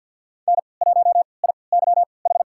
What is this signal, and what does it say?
Morse code, heard as a keyed tone, about 48 wpm, spaced as for 28 wpm, 700 Hz. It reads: N1IYH
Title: USA Call Signs - 5 Characters - 3x Character Spacing - 50wpm